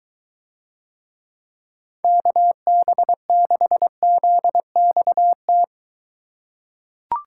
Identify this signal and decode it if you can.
Morse code, heard as a keyed tone, about 23 wpm, 700 Hz. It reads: KB6ZXT